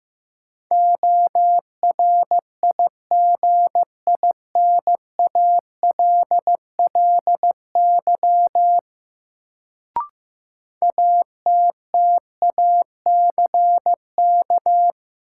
Morse code, heard as a keyed tone, about 15 wpm, 700 Hz. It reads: ORIGINALLY ATTACK
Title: Sentences from Top 1500 Words - New Word Drill - 15wpm